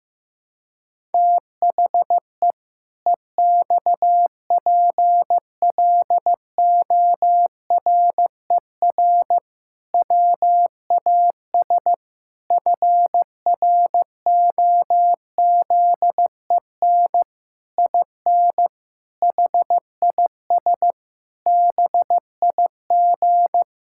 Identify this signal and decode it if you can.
Morse code, heard as a keyed tone, about 15 wpm, 700 Hz. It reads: THE EXPLORER WAS FROZEN IN HIS BIG
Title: Pangram Challenge v2 - 15wpm